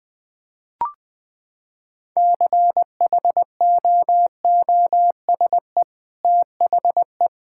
Morse code, heard as a keyed tone, about 21 wpm, 700 Hz. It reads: CHOOSE THE
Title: Sentences with 2-Word Phrases - 20wpm